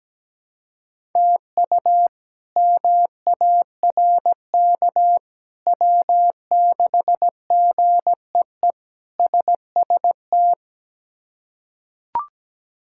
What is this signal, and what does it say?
Morse code, heard as a keyed tone, about 17 wpm, 700 Hz. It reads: TU MARK W6GEE SST
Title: K1USN SST QSO Practice - 17wpm